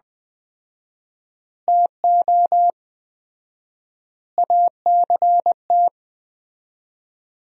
Morse code, heard as a keyed tone, about 20 wpm, 700 Hz. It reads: TO ACT
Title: 4x Word-Spacing - Sentences from Top 500 Words - 20wpm